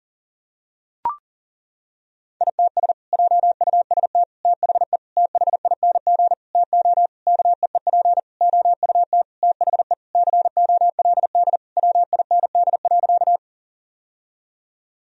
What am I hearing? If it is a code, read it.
Morse code, heard as a keyed tone, about 40 wpm, 700 Hz. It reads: ITS JUST THE THING TO KEEP OUT THE COLD WIND.
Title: Sentences from Top 1000 Words - Review Part 2 - 40wpm